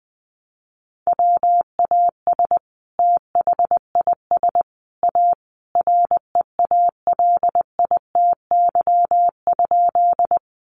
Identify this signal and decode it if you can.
Morse code, heard as a keyed tone, about 20 wpm, 700 Hz. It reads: WAS THIS A REALITY?
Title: Sentences from Top 1200 Words 20wpm